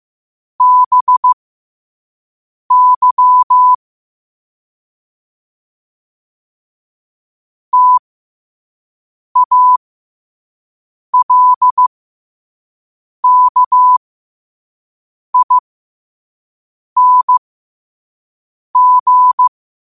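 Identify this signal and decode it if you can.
Morse code, heard as a keyed tone, about 15 wpm, spaced as for 5 wpm, 1000 Hz. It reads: BY TALKING
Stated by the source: Morse quote of the day at 5 wpm.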